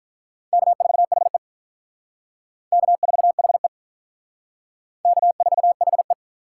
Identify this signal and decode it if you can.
Morse code, heard as a keyed tone, about 44 wpm, 700 Hz. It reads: K4HE K4HE K4HE